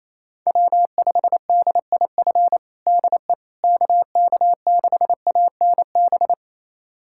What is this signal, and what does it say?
Morse code, heard as a keyed tone, about 27 wpm, 700 Hz. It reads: W5DIF DE KK6ANB
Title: USA Callsign DE Callsign 28wpm